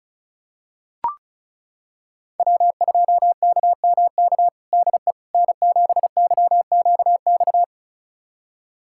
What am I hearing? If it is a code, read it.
Morse code, heard as a keyed tone, about 36 wpm, 700 Hz. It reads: W2KMK DE N7YQX
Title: USA Callsign DE Callsign - 5 Characters - 35wpm